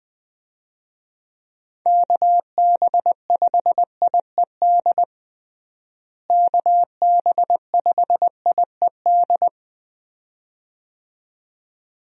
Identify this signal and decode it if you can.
Morse code, heard as a keyed tone, about 20 wpm, 700 Hz. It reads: KB5IED KB5IED
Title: USA Call Signs - 6 Characters - Repeated 2x - 20wpm